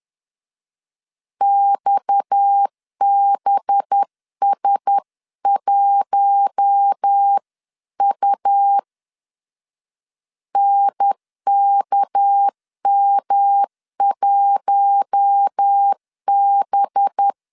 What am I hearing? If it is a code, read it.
Morse code, heard as a keyed tone, about 10 wpm, 800 Hz. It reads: XBS1U NKM1B
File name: CW / DET / 1 6 1500C S B 10wpm 4137